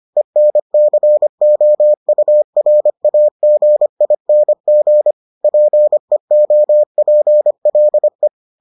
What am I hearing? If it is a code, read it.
Morse code, heard as a keyed tone, about 25 wpm, 600 Hz. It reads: ENCOURAGING PEOPLE